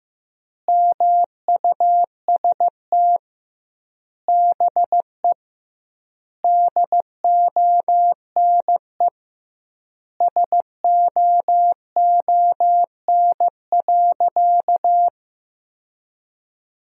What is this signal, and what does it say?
Morse code, heard as a keyed tone, about 15 wpm, 700 Hz. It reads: MUST BE DONE SOON.
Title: 2x Word-Spacing - Sentences from Top 500 Words - 15wpm